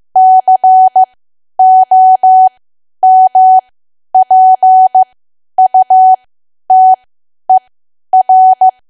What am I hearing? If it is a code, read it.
Morse code, heard as a keyed tone, about 15 wpm, spaced as for 10 wpm, 750 Hz. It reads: COMPUTER
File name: cwa / W1AW-Level2 / 10WPM